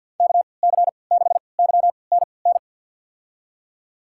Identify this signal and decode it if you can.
Morse code, heard as a keyed tone, about 53 wpm, spaced as for 29 wpm, 700 Hz. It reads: KC6YNN
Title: USA Call Signs - 6 Characters - 3x Character Spacing - 50wpm